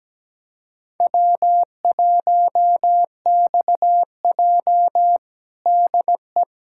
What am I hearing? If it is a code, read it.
Morse code, heard as a keyed tone, about 17 wpm, 700 Hz. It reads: W1XJ DE
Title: USA Callsign DE Callsign - 4 Characters - 17wpm